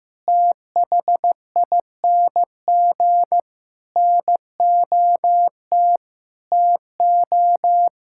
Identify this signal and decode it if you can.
Morse code, heard as a keyed tone, about 15 wpm, 700 Hz. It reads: THING NOT TO
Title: Sentences from Top 1000 Words - Review Part 6 - 15wpm